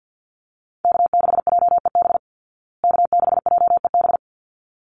Morse code, heard as a keyed tone, about 50 wpm, 700 Hz. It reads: K6JEB K6JEB
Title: Call Signs Repeated 3x - 50wpm